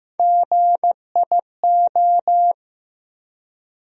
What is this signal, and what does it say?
Morse code, heard as a keyed tone, about 15 wpm, 700 Hz. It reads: GIO
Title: USA Call Sign Postfixes - 3 Letters 15wpm